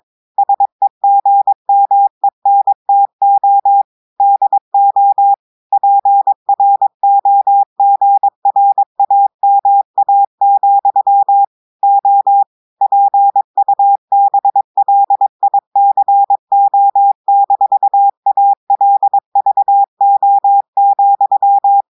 Morse code, heard as a keyed tone, about 22 wpm, 800 Hz. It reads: SEGMENTO DO PROGRAMA, O PUBLICO-ALVO,
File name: audios / portugais / portugais texte en clair 2 vitesse 22 mots minutes